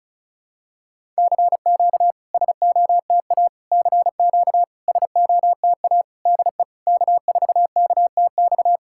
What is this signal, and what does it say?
Morse code, heard as a keyed tone, about 39 wpm, 700 Hz. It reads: CQ SOTA CQ SOTA DE K4KTX
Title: SOTA QSO Practice 35wpm